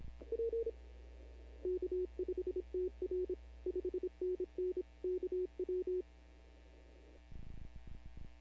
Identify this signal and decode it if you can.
Morse code, heard as a keyed tone, about 26 wpm, 370 Hz. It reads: K5TR 5NN KW
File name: CW / 2005-k5tr arrl cw dx -15:00